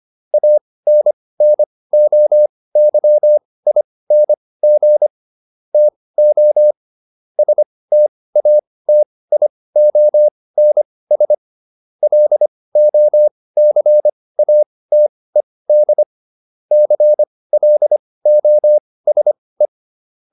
Morse code, heard as a keyed tone, about 25 wpm, spaced as for 18 wpm, 600 Hz. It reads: ANNOYING TO STATIONS LOCATED CLOSE